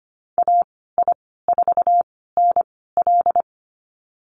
Morse code, heard as a keyed tone, about 25 wpm, spaced as for 16 wpm, 690 Hz. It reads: AI4NL
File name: Lesson 15g - Number 5 - Call Signs 25wpm